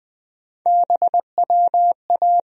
Morse code, heard as a keyed tone, about 21 wpm, 700 Hz. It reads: BWA
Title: USA Call Sign Postfixes - 3 Letters 20wpm